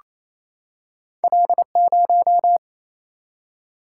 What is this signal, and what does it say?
Morse code, heard as a keyed tone, about 28 wpm, 700 Hz. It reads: L0
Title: Single-Letter Single-Number 28wpm